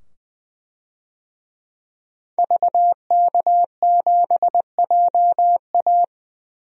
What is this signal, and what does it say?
Morse code, heard as a keyed tone, about 21 wpm, 700 Hz. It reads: VK7JA
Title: DXCC-100 Call Signs - 5 Characters - 20wpm